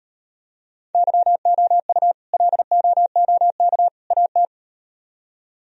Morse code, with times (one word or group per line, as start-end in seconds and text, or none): 0.9-2.1 YOU
2.3-3.9 LOOK
4.1-4.5 AT